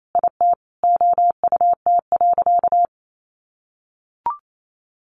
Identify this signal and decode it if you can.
Morse code, heard as a keyed tone, about 28 wpm, 700 Hz. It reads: IT OUT.
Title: Sentences from Top 1500 Words 28wpm